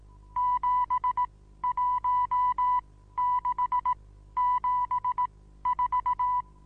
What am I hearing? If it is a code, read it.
Morse code, heard as a keyed tone, about 18 wpm, spaced as for 13 wpm, 990 Hz. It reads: 71674
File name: Onlinelehrgang / morsen / DJ4UF Morsen25